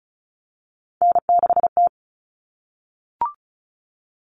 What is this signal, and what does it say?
Morse code, heard as a keyed tone, about 34 wpm, 700 Hz. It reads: N6T
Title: All Cut Numbers - 3-Digit 35wpm